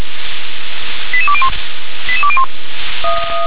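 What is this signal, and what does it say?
Five-tone selective call (ZVEI), heard as quick runs of five tones, tones of about 80 ms. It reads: 88201 88201
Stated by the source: ZVEI five-tone selective call 88119, received by SDR.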